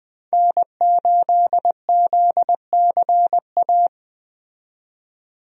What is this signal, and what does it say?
Morse code, heard as a keyed tone, about 20 wpm, 700 Hz. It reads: N8ZCA
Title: USA Call Signs - 5 Characters - 20wpm